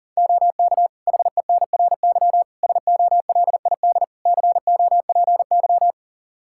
Morse code, heard as a keyed tone, about 40 wpm, 700 Hz. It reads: OK HENRY SOLID COPY
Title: Practice QSO - Exchange 3 - More Info - 40wpm